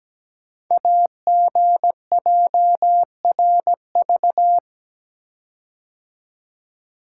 Morse code, heard as a keyed tone, about 17 wpm, 700 Hz. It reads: AGJRV
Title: Code Groups - 5 Letter:Numbers - 17wpm